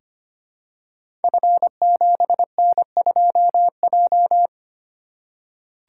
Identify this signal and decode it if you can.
Morse code, heard as a keyed tone, about 25 wpm, 700 Hz. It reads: F7N2J
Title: Code Groups - 5 Letter:Numbers - 25wpm